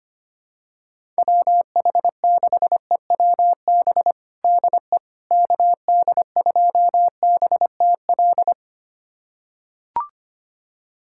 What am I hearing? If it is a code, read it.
Morse code, heard as a keyed tone, about 25 wpm, 700 Hz. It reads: WH6EWB DE KD2BTL
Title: USA Callsign DE Callsign - 6 Characters - 25wpm